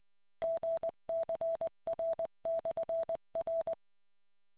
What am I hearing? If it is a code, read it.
Morse code, heard as a keyed tone, about 20 wpm, 660 Hz. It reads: GCR/R